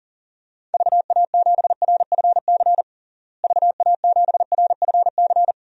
Morse code, heard as a keyed tone, about 40 wpm, 700 Hz. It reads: VA7RFC VA7RFC
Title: DXCC-100 Call Signs - 6 Characters - Repeated 3x - 40wpm